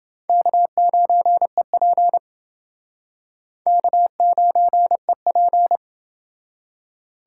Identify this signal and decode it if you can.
Morse code, heard as a keyed tone, about 28 wpm, 700 Hz. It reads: K9EP K9EP